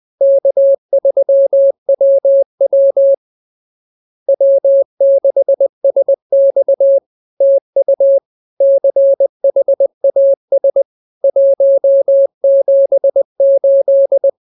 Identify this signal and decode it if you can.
Morse code, heard as a keyed tone, about 20 wpm, 550 Hz. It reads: K3WW W6SX TU CHAS 178